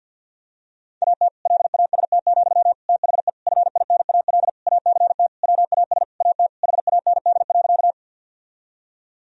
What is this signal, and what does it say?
Morse code, heard as a keyed tone, about 46 wpm, 700 Hz. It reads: AT LAST, THE FINAL ACT WAS AT HAND.
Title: Sentences from Top 1000 Words - Review Part 2 - 50wpm